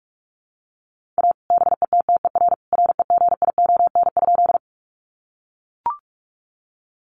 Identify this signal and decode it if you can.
Morse code, heard as a keyed tone, about 44 wpm, 700 Hz. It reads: A BETTER REGION?
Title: Sentences from Top 1500 Words 45wpm